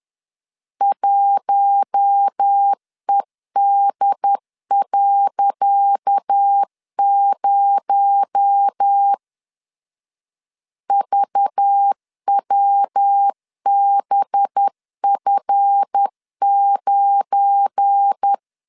Morse code, heard as a keyed tone, about 10 wpm, 800 Hz. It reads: 1ED.0 VWBF9
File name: LEZIONI / 10 SIGNS / 10C DOT 10wpm 2940